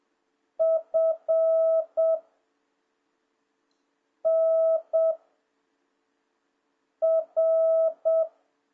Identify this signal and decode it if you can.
Morse code, heard as a keyed tone, about 7 wpm, 660 Hz. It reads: F N R